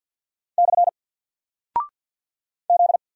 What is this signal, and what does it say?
Morse code, heard as a keyed tone, about 48 wpm, 700 Hz. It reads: / Z
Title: Single Character (Letter:Number:Punctuation:Prosign) Alt 50wpm